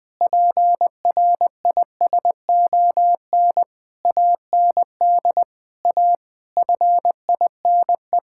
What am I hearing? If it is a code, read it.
Morse code, heard as a keyed tone, about 20 wpm, 700 Hz. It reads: PRISON AND A FINE
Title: Sets of 9 Words - 20wpm